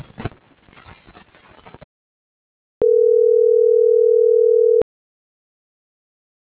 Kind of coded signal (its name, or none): telephone call progress tone